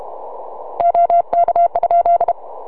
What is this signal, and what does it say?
Morse code, heard as a keyed tone, about 33 wpm, 690 Hz. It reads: OK?
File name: sy9vhf 6m